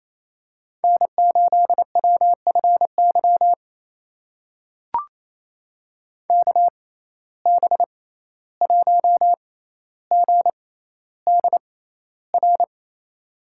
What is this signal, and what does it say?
Morse code, heard as a keyed tone, about 28 wpm, spaced as for 14 wpm, 700 Hz. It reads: N8WFY KB1GDR
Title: USA Call Signs - 6x Character Spacing - 28wpm